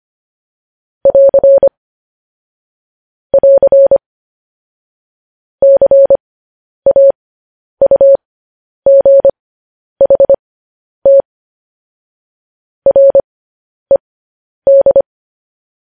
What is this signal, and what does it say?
Morse code, heard as a keyed tone, about 25 wpm, spaced as for 10 wpm, 550 Hz. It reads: <AR> <AR> CAUGHT RED
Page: News Headlines in Morse Code at 10 WPM